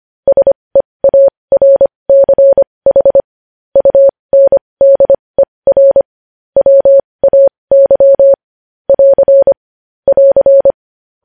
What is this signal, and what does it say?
Morse code, heard as a keyed tone, about 25 wpm, spaced as for 20 wpm, 550 Hz. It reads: SEARCH UNDER WAY <AR> <AR>
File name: podcast / News headlines-1x-25x20